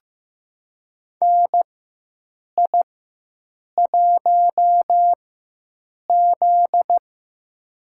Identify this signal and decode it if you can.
Morse code, heard as a keyed tone, about 15 wpm, spaced as for 7 wpm, 700 Hz. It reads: NI1Z
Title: USA Call Signs - 4 Characters - 4x Character Spacing - 15wpm